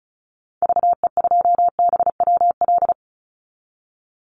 Morse code, heard as a keyed tone, about 34 wpm, 700 Hz. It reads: VE2BWL
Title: Call Signs - 35wpm